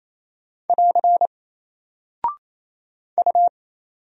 Morse code, heard as a keyed tone, about 27 wpm, 700 Hz. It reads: <AR> U
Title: Single Character (Letter:Number:Punctuation:Prosign) 28wpm